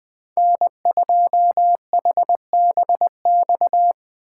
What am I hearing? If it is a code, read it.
Morse code, heard as a keyed tone, about 21 wpm, 700 Hz. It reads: N2HBX